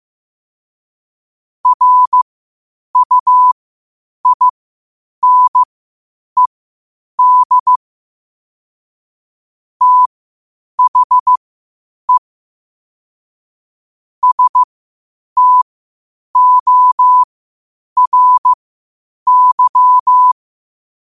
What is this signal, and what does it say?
Morse code, heard as a keyed tone, about 15 wpm, spaced as for 8 wpm, 1000 Hz. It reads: RUINED THE STORY
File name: qotdfiles / 8wpm